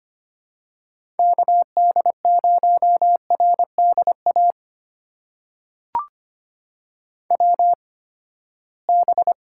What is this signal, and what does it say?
Morse code, heard as a keyed tone, about 25 wpm, spaced as for 17 wpm, 700 Hz. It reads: KD0RDA WB
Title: USA Call Signs - 6 Characters - 8x Character Spacing - 25wpm